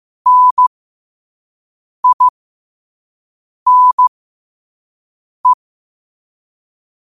Morse code, heard as a keyed tone, about 15 wpm, spaced as for 5 wpm, 1000 Hz. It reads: NINE
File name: qotdfiles / 5wpm